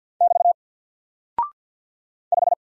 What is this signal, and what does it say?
Morse code, heard as a keyed tone, about 48 wpm, 700 Hz. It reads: <BT> 5
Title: Single Character (Letter:Number:Punctuation:Prosign) Alt 50wpm